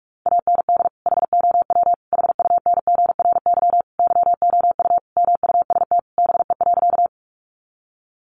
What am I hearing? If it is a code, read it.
Morse code, heard as a keyed tone, about 44 wpm, 700 Hz. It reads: AND HOW HUNGRY YOU MUST BE.
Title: Sentences from Top 1400 Words 45wpm